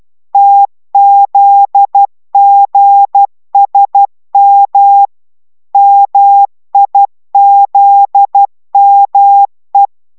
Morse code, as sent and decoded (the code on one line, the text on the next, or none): - --.. --. ... -- / -- .. --.. -- .
TZGSM MIZME